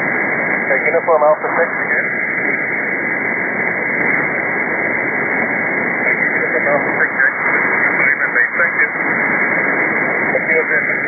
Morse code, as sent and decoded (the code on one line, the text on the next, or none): none